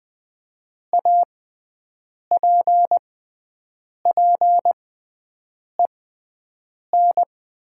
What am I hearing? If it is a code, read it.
Morse code, heard as a keyed tone, about 20 wpm, spaced as for 7 wpm, 700 Hz. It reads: APPEN
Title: 6x Character Spacing - Top 500 Words - 20wpm